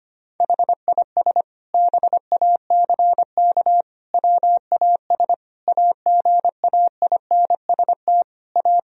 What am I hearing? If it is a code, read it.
Morse code, heard as a keyed tone, about 25 wpm, 700 Hz. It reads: HIS BACK WAS AGAINST A
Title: Sentences from Top 1000 Words - Review Part 1 - 25wpm